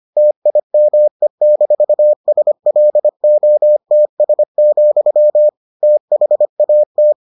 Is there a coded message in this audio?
Yes: Morse code from 0.2 to 7.1 s, heard as a keyed tone, about 25 wpm, 600 Hz. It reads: TIME-SLOTS, THAT